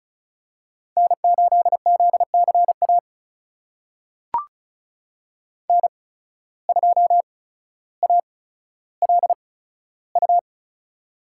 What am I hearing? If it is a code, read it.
Morse code, heard as a keyed tone, about 34 wpm, spaced as for 15 wpm, 700 Hz. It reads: N8ZCA N2ALU
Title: USA Call Signs - 5 Characters - 8x Character Spacing - 35wpm